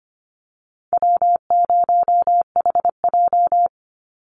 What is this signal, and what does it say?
Morse code, heard as a keyed tone, about 25 wpm, 700 Hz. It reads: W0HJ
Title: USA Call Signs 25wpm